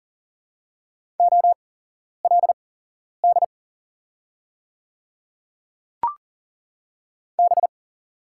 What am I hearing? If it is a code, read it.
Morse code, heard as a keyed tone, about 40 wpm, spaced as for 11 wpm, 700 Hz. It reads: OLD B
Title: 8x Character Spacing - Top 200 Words - 40wpm